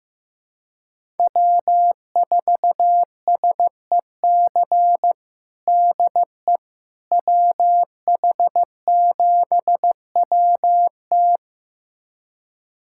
Morse code, heard as a keyed tone, about 15 wpm, 700 Hz. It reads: W4SEC DE WH7WT
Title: USA Callsign DE Callsign - 5 Characters - 15wpm